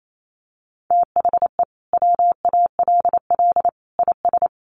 Morse code, heard as a keyed tone, about 28 wpm, 700 Hz. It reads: THE WALL IS
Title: Sentences from Top 1400 Words 28wpm